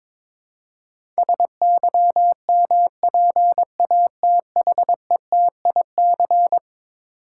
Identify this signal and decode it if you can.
Morse code, heard as a keyed tone, about 21 wpm, 700 Hz. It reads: SYMPATHETIC
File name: practice / 22wpm / Eleven Letter Words - 22wpm